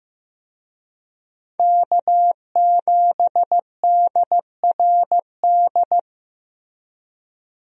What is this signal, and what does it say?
Morse code, heard as a keyed tone, about 15 wpm, 700 Hz. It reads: K7DRD
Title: USA Call Signs - 5 Characters - 15wpm